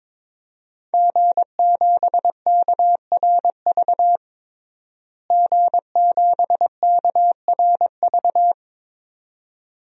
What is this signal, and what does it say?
Morse code, heard as a keyed tone, about 22 wpm, 700 Hz. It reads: G7KRV G7KRV
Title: DXCC-100 Call Signs - 5 Characters - Repeated 2x - 22wpm